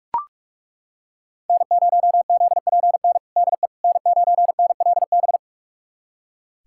Morse code, heard as a keyed tone, about 41 wpm, 700 Hz. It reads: N0ZPN DE N9NLB
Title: USA Callsign DE Callsign 45wpm